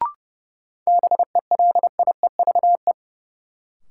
Morse code, heard as a keyed tone, about 31 wpm, 700 Hz. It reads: BELIEVE
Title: Sentences from Top 1000 Words - New Word Drill - 30wpm